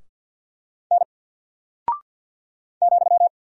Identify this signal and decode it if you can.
Morse code, heard as a keyed tone, about 48 wpm, 700 Hz. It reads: N ,
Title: Single Character (Letter:Number:Punctuation:Prosign) Alt 50wpm